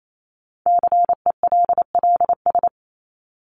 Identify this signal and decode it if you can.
Morse code, heard as a keyed tone, about 28 wpm, 700 Hz. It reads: CELLS
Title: ES:S Suffix - Top 100 Words with Base Words 28wpm